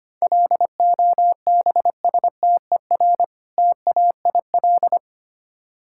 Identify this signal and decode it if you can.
Morse code, heard as a keyed tone, about 25 wpm, 700 Hz. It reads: LOBSTER TAIL
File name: practice / 25wpm / Difficult Words with Hints - 25wpm